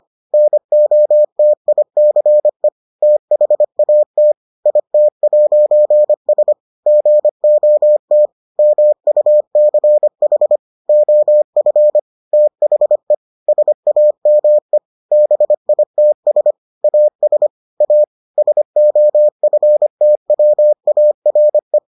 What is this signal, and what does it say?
Morse code, heard as a keyed tone, about 25 wpm, 600 Hz. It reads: NOTICE THAT IT'S GOT MUCH OF THE SAME BITS AS A SOFTWARE